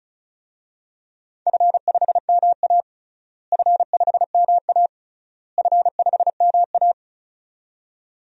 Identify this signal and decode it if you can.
Morse code, heard as a keyed tone, about 34 wpm, 700 Hz. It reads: F5MA F5MA F5MA
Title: DXCC-100 Call Signs - 4 Characters - Repeated 3x - 35wpm